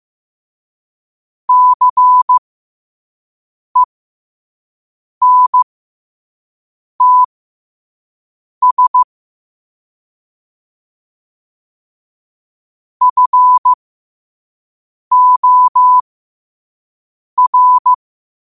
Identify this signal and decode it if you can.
Morse code, heard as a keyed tone, about 15 wpm, spaced as for 5 wpm, 1000 Hz. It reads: CENTS FOR